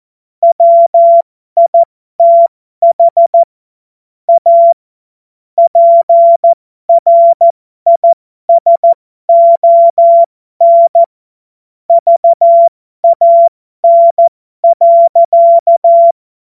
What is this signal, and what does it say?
Morse code, heard as a keyed tone, about 14 wpm, spaced as for 12 wpm, 680 Hz. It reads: WITH A PRISON VAN.
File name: Morsekurs / Witze